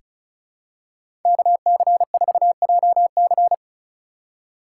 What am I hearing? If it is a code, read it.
Morse code, heard as a keyed tone, about 36 wpm, 700 Hz. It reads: KC4JC
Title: USA Call Signs - 3x Character Spacing - 35wpm